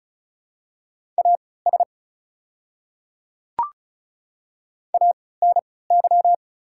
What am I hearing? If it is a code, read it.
Morse code, heard as a keyed tone, about 36 wpm, spaced as for 20 wpm, 700 Hz. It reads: AS ANY
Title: 3x Character Spacing - Top 200 Words - 35wpm